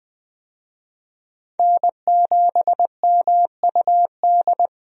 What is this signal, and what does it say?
Morse code, heard as a keyed tone, about 20 wpm, 700 Hz. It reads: N7MUD
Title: USA Call Signs - 5 Characters - Repeated 2x - 20wpm